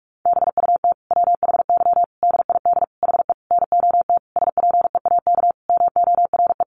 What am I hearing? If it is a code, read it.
Morse code, heard as a keyed tone, about 44 wpm, 700 Hz. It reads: BUT WHY DID HE NOT SPEAK MORE
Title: Sentences from Top 1300 Words 45wpm